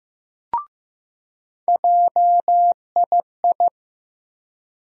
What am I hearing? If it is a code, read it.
Morse code, heard as a keyed tone, about 15 wpm, 700 Hz. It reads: JII
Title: USA Call Sign Postfixes - 3 Letters 15wpm